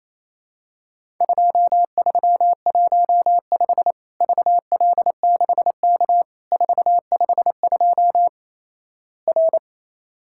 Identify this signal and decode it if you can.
Morse code, heard as a keyed tone, about 28 wpm, 700 Hz. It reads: 2315 VL6K 452 R
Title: Worked All Europe DX Contest - QTC Exchange - 28wpm